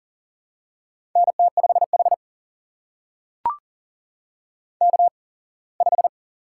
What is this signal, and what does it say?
Morse code, heard as a keyed tone, about 40 wpm, spaced as for 24 wpm, 700 Hz. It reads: NT5H K5
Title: USA Call Signs - 8x Character Spacing - 40wpm